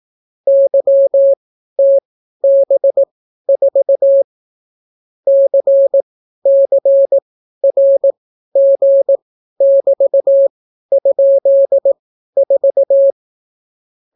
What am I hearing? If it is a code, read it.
Morse code, heard as a keyed tone, about 18 wpm, spaced as for 12 wpm, 550 Hz. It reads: YTB4 CCRG<BT>?4